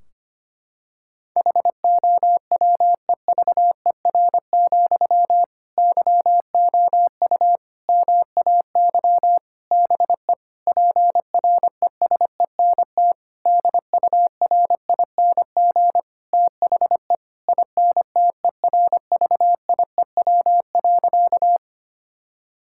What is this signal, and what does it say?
Morse code, heard as a keyed tone, about 25 wpm, 700 Hz. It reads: HOWEVER, YOU MAY BE PRESENT DURING THE INTERVIEW.
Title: Sentences from Top 1600 Words 25wpm